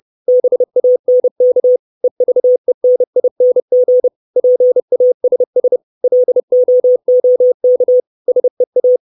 Morse code, heard as a keyed tone, about 30 wpm, 500 Hz. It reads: BANK EVENING PASS LOOK SEA
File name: 632 Words 30WPM